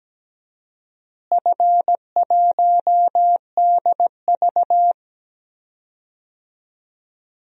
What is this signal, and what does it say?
Morse code, heard as a keyed tone, about 17 wpm, 700 Hz. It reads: F1DV
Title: DXCC-100 Call Signs - 4 Characters - Repeated 2x - 17wpm